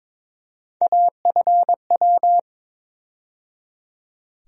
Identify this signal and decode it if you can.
Morse code, heard as a keyed tone, about 22 wpm, 700 Hz. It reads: AFW
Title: USA Call Sign Postfixes - 3 Letters 22wpm